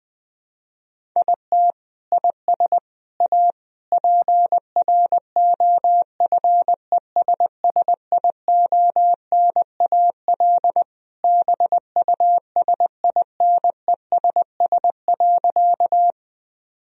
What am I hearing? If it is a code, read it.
Morse code, heard as a keyed tone, about 20 wpm, 700 Hz. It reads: IT IS A PROFESSIONAL BUSINESS.
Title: Sentences from Top 1000 Words - Review Part 4 - 20wpm